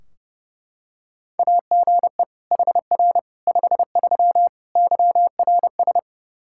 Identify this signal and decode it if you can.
Morse code, heard as a keyed tone, about 30 wpm, 700 Hz. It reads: AGE HR 53 YRS
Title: Practice QSO - Exchange 3 - More Info - 30wpm